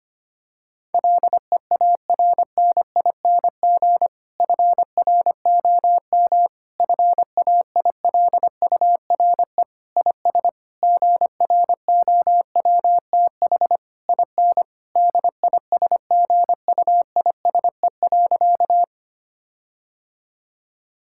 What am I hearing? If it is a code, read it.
Morse code, heard as a keyed tone, about 25 wpm, 700 Hz. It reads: LEARNING FROM FAILURE IS GROWTH IN DISGUISE.
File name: practice / 25wpm / Difficult Sentences with Hints - 25wpm